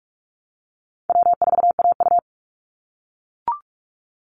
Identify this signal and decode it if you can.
Morse code, heard as a keyed tone, about 44 wpm, 700 Hz. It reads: W4AU
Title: Call Signs - 45wpm